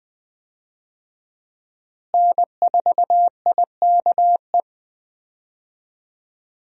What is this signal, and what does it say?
Morse code, heard as a keyed tone, about 20 wpm, 700 Hz. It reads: N4IKE
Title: USA Call Signs - 5 Characters - Repeated 3x - 20wpm